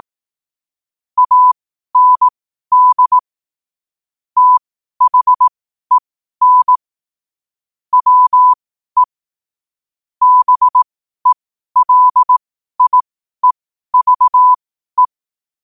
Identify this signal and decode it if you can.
Morse code, heard as a keyed tone, about 18 wpm, spaced as for 12 wpm, 1000 Hz. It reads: AND THEN WE BELIEVE
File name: qotdfiles / 12wpm